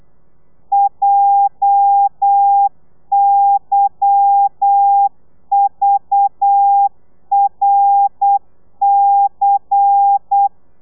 Morse code, heard as a keyed tone, about 8 wpm, 800 Hz. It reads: JYVRC
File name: telegrafia / i2tfj / Lez10 35